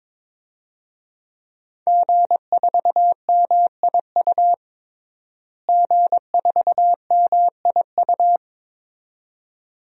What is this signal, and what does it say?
Morse code, heard as a keyed tone, about 22 wpm, 700 Hz. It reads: G4MIU G4MIU
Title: DXCC-100 Call Signs - 5 Characters - Repeated 2x - 22wpm